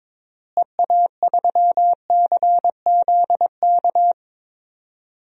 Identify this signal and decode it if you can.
Morse code, heard as a keyed tone, about 21 wpm, 700 Hz. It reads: EA3CZK
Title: DXCC-100 Call Signs - 6 Characters - Repeated 2x - 22wpm